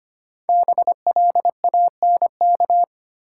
Morse code, heard as a keyed tone, about 25 wpm, 700 Hz. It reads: BLANK
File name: 5 Letter Words - First 3 Letters Given - 25wpm